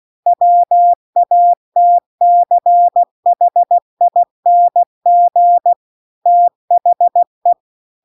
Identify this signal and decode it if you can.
Morse code, heard as a keyed tone, about 16 wpm, 700 Hz. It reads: WATCHING THE